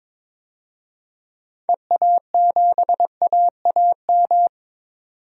Morse code, heard as a keyed tone, about 22 wpm, 700 Hz. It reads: EA7AAM
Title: DXCC-100 Call Signs - 6 Characters - 22wpm